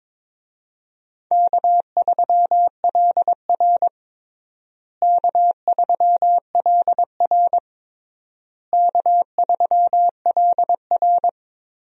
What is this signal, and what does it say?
Morse code, heard as a keyed tone, about 22 wpm, 700 Hz. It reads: K3LR K3LR K3LR
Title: USA Call Signs - 4 Characters - Repeated 3x - 22wpm